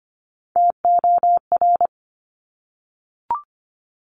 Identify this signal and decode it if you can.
Morse code, heard as a keyed tone, about 25 wpm, 700 Hz. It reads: TOR
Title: Top Three Letter Combinations in Words 25wpm